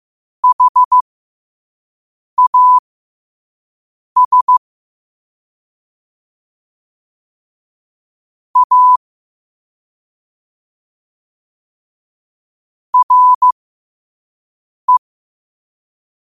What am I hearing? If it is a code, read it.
Morse code, heard as a keyed tone, about 15 wpm, spaced as for 5 wpm, 1000 Hz. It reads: HAS A RE
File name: qotdfiles / 5wpm